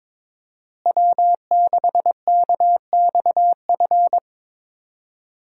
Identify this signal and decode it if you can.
Morse code, heard as a keyed tone, about 22 wpm, 700 Hz. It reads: W6KXF
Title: USA Call Signs - 5 Characters - Repeated 2x - 22wpm